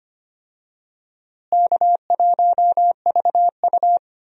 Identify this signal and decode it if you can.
Morse code, heard as a keyed tone, about 25 wpm, 700 Hz. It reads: K1VU
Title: USA Call Signs - 4 Characters - Repeated 3x - 25wpm